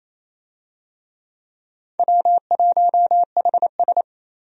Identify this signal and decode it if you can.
Morse code, heard as a keyed tone, about 27 wpm, 700 Hz. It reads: W1HS
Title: USA Call Signs - 4 Characters - 28wpm